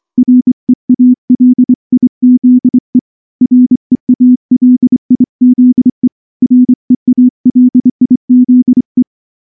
Morse code, heard as a keyed tone, about 24 wpm, 250 Hz. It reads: REALIZE REALIZE REALIZE
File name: morse / morse-training-25WPM